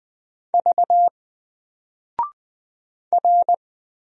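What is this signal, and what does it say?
Morse code, heard as a keyed tone, about 20 wpm, 700 Hz. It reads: V R
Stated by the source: Single letter and number Morse drill at 20 wpm, from N4LCD.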